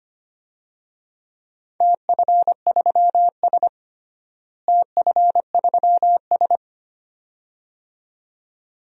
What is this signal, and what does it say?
Morse code, heard as a keyed tone, about 25 wpm, 700 Hz. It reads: TF3S TF3S
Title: DXCC-100 Call Signs - 4 Characters - Repeated 2x - 25wpm